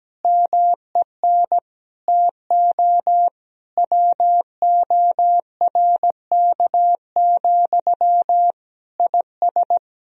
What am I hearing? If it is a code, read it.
Morse code, heard as a keyed tone, about 17 wpm, 700 Hz. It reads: MEN TO WORK, IS
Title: Sentences from Top 1000 Words - Review Part 2 - 17wpm